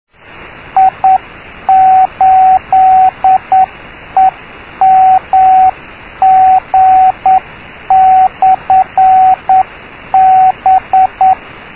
Morse code, heard as a keyed tone, about 9 wpm, 760 Hz, with radio noise. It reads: I8EMG/B
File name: i5wbe / i8emg b